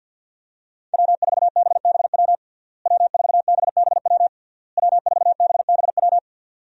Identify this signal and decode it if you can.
Morse code, heard as a keyed tone, about 50 wpm, 700 Hz. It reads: W4BBW W4BBW W4BBW
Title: USA Call Signs - 5 Characters - Repeated 3x - 50wpm